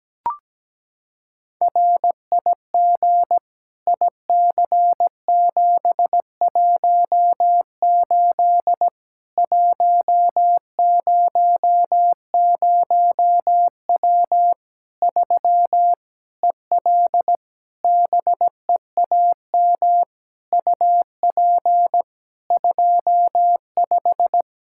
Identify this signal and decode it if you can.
Morse code, heard as a keyed tone, about 17 wpm, 700 Hz. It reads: RIG IC718 100W 3 EL BEAM UP 25
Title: Practice QSO - Exchange 2 - Info - 17wpm